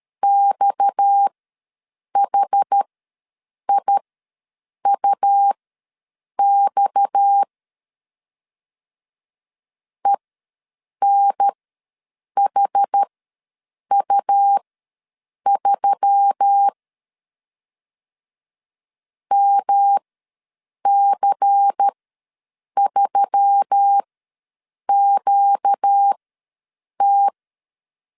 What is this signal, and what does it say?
Morse code, heard as a keyed tone, about 13 wpm, spaced as for 7 wpm, 800 Hz. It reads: XHIUX ENHU3 MC3QT
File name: LEZIONI / 6 PKX0 / 6 PKX ws 12wpm 3106